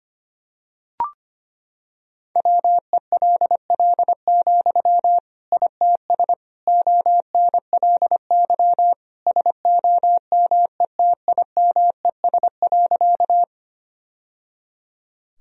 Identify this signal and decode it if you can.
Morse code, heard as a keyed tone, about 25 wpm, 700 Hz. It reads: WELL, ITS ONLY SOMETIMES.
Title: Sentences from Top 1000 Words - Review Part 3 - 25wpm